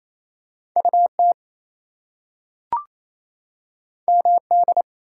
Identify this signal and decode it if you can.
Morse code, heard as a keyed tone, about 28 wpm, 700 Hz. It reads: UT MD
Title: US States and Canadian Provinces 28wpm